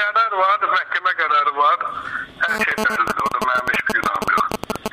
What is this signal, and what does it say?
Emergency Alert System (SAME) tones, heard as harsh buzzing bursts, with speech. It is heard from 0.8 to 4.0 s.